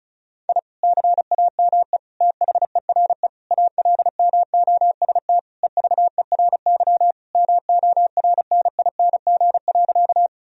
Morse code, heard as a keyed tone, about 36 wpm, 700 Hz. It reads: I CAME THERE ALMOST EVERY MORNING.
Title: Sentences from Top 1000 Words - Review Part 4 - 35wpm